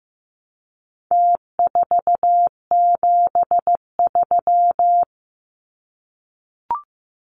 Morse code, heard as a keyed tone, about 15 wpm, 700 Hz. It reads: T473
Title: Numbers - Contest Style - 4-Digit 15wpm